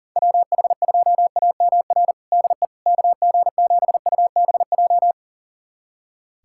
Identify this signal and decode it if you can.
Morse code, heard as a keyed tone, about 40 wpm, 700 Hz. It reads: WH2AMR DE KG7UBJ
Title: USA Callsign DE Callsign - 6 Characters - 40wpm